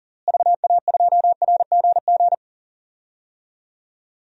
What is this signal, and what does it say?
Morse code, heard as a keyed tone, about 40 wpm, 700 Hz. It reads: VA2RGG
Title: DXCC-100 Call Signs - 6 Characters - Repeated 2x - 40wpm